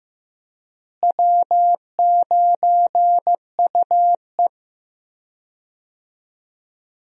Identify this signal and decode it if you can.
Morse code, heard as a keyed tone, about 15 wpm, 700 Hz. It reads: W9UE
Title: USA Call Signs - 4 Characters - Repeated 3x - 15wpm